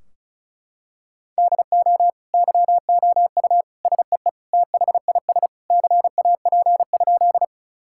Morse code, heard as a keyed tone, about 36 wpm, 700 Hz. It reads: DO YOU SEE THIS CAP?
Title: Sentences from Top 1600 Words 35wpm